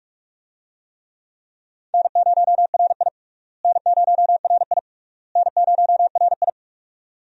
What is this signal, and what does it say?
Morse code, heard as a keyed tone, about 46 wpm, 700 Hz. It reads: N0RI N0RI N0RI
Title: USA Call Signs - 4 Characters - Repeated 3x - 45wpm